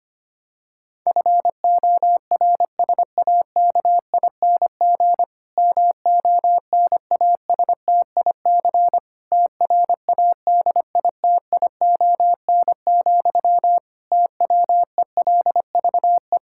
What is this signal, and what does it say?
Morse code, heard as a keyed tone, about 25 wpm, 700 Hz. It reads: FORSAKING MONASTIC TRADITION, TWELVE
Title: Pangram Challenge v2 - 25wpm